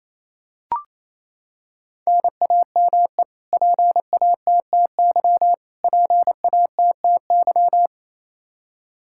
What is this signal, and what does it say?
Morse code, heard as a keyed tone, about 28 wpm, 700 Hz. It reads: NAME PATTY PATTY
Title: Practice QSO - Exchange 1 - Beginning - 28wpm